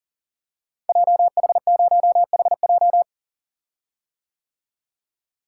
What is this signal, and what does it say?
Morse code, heard as a keyed tone, about 40 wpm, 700 Hz. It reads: JH0HJ
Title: Code Groups - 5 Letter:Numbers - 40wpm